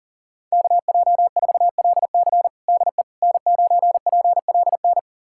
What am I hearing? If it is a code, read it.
Morse code, heard as a keyed tone, about 40 wpm, 700 Hz. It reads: KJ4LC DE N9PLN
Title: USA Callsign DE Callsign - 5 Characters - 40wpm